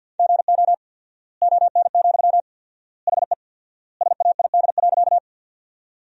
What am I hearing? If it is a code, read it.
Morse code, heard as a keyed tone, about 50 wpm, 700 Hz. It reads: GO ON, HE SAID.
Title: 4x Word-Spacing - Sentences from Top 100 Words - 50wpm